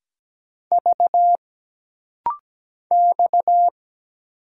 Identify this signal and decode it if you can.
Morse code, heard as a keyed tone, about 17 wpm, 700 Hz. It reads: V X
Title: Single Letter-Number - Rapid-Fire 17wpm